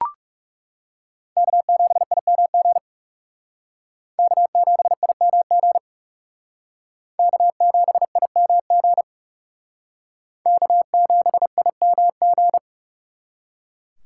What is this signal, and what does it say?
Morse code, heard as a keyed tone, about 37 wpm, 700 Hz. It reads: K7IMG K7IMG K7IMG K7IMG
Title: Speed Racing - USA 1x3 Call Signs 30wpm